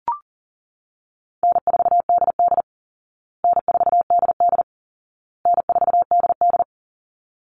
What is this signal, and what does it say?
Morse code, heard as a keyed tone, about 40 wpm, 700 Hz. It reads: N4DD N4DD N4DD
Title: Call Signs Repeated 2x - 40wpm